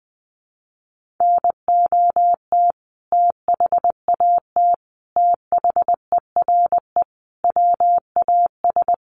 Morse code, heard as a keyed tone, about 20 wpm, 700 Hz. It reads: NOT THAT THERE WAS
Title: Sentences from Top 1400 Words 20wpm